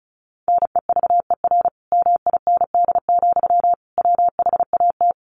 Morse code, heard as a keyed tone, about 34 wpm, 700 Hz. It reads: NEVER MIND, WHAT
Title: Sentences from Top 700 Words 35wpm